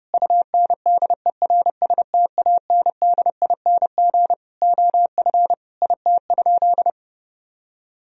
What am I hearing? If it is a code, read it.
Morse code, heard as a keyed tone, about 29 wpm, 700 Hz. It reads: UNDERSTANDING OF IT?
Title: Sentences from Top 1000 Words - Review Part 3 - 30wpm